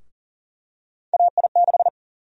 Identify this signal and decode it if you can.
Morse code, heard as a keyed tone, about 40 wpm, 700 Hz. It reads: AI6
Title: USA Call Sign Prefixes - Repeated 2x - 40wpm